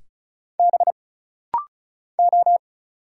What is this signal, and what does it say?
Morse code, heard as a keyed tone, about 36 wpm, 700 Hz. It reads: B O
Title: Single Letters - Rapid-Fire 35wpm